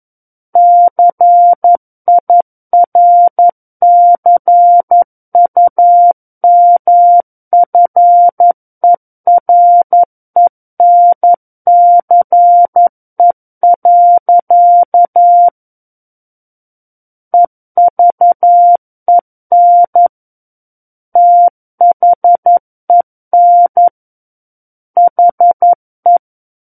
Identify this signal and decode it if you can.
Morse code, heard as a keyed tone, about 11 wpm, 700 Hz. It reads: CIRCUMFERENCE. EVEN THEN HE
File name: Book One / 02-Chapter 2 - 11 WPM